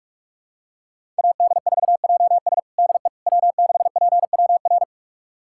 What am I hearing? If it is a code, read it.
Morse code, heard as a keyed tone, about 42 wpm, 700 Hz. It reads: AD4JS DE W6PWR
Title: USA Callsign DE Callsign - 5 Characters - 45wpm